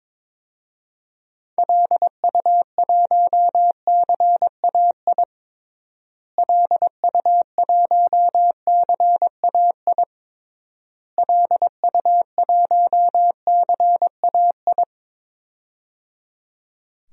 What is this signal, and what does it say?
Morse code, heard as a keyed tone, about 22 wpm, 700 Hz. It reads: LU1CAI LU1CAI LU1CAI
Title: DXCC-100 Call Signs - 6 Characters - Repeated 3x - 22wpm